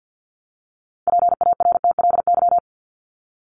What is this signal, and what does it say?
Morse code, heard as a keyed tone, about 50 wpm, 700 Hz. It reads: PARTLY
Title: LY Suffix - Top 100 Words with Base Words 50wpm